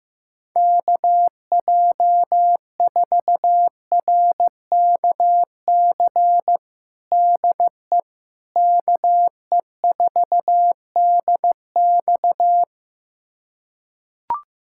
Morse code, heard as a keyed tone, about 15 wpm, 700 Hz. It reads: KJ4RKC DE KE4DX
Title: USA Callsign DE Callsign 15wpm